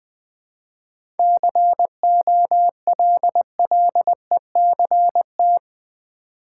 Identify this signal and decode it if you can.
Morse code, heard as a keyed tone, about 20 wpm, 700 Hz. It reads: COLLECT
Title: Seven Letter Words - 20wpm